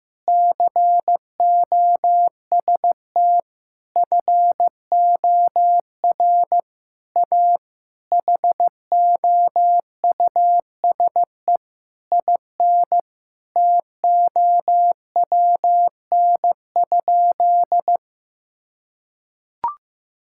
Morse code, heard as a keyed tone, about 15 wpm, 700 Hz. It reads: COST FOR A HOUSE IN TOWN?
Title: Sentences from Top 1000 Words - Review Part 6 - 15wpm